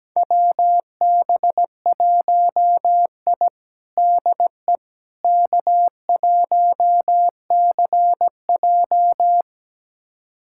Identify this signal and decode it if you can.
Morse code, heard as a keyed tone, about 17 wpm, 700 Hz. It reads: WB1I DE K1CJ
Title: USA Callsign DE Callsign - 4 Characters - 17wpm